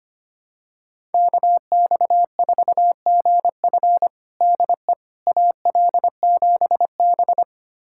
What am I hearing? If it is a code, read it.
Morse code, heard as a keyed tone, about 25 wpm, 700 Hz. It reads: KX4GF DE AL7B
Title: USA Callsign DE Callsign 25wpm